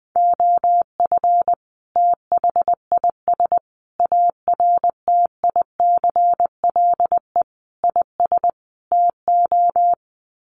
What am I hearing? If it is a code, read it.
Morse code, heard as a keyed tone, about 20 wpm, 700 Hz. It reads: OF THIS ARTICLE IS TO
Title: Sets of 5 Words Encore2 20wpm